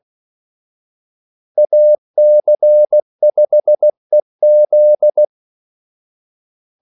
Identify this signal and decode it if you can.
Morse code, heard as a keyed tone, about 16 wpm, 600 Hz. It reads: AC5EZ
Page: Kaldesignaler 16 wpm